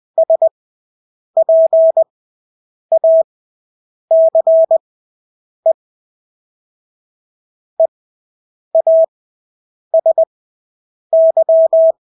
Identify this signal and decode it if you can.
Morse code, heard as a keyed tone, about 20 wpm, spaced as for 8 wpm, 650 Hz. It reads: SPACE EASY
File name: Engelske ord 20/8 wpm